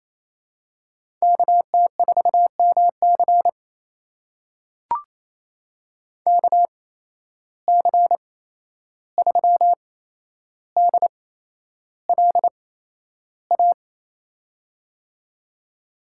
Morse code, heard as a keyed tone, about 28 wpm, spaced as for 11 wpm, 700 Hz. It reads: KT4MC KC3DLA
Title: USA Call Signs - 8x Character Spacing - 28wpm